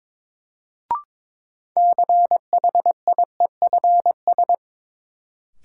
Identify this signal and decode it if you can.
Morse code, heard as a keyed tone, about 22 wpm, 700 Hz. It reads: CHIEFS
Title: Sets of 5 Words Encore - New Word Drill - 22wpm